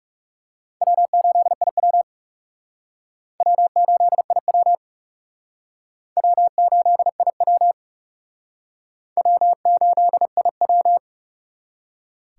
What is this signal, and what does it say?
Morse code, heard as a keyed tone, about 36 wpm, 700 Hz. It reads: W8IW W8IW W8IW W8IW
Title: Speed Racing - USA 1x2 Call Signs 30wpm